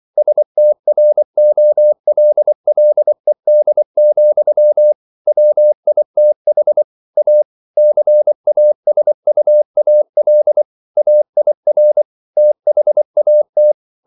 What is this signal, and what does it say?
Morse code, heard as a keyed tone, about 24 wpm, 600 Hz. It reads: STROLLED, WITH A CASUAL AIR THAT